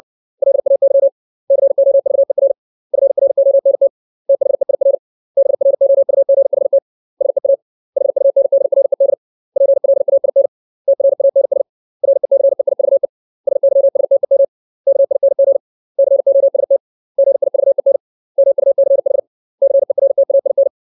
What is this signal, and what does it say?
Morse code, heard as a keyed tone, about 60 wpm, 550 Hz. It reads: LAY COVER FRONT THEIR BROUGHT SEA HUNDRED PLAIN TRANS RECEIVE SYSTEM KING COST GIVEN MUCH CERTAIN
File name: 632 words / 632 Words 60WPM